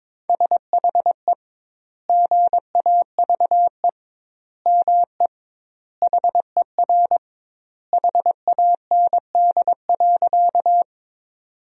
Morse code, heard as a keyed tone, about 22 wpm, 700 Hz. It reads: SHE GAVE ME HER HAND.
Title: 2x Word-Spacing - Sentences from Top 500 Words - 22wpm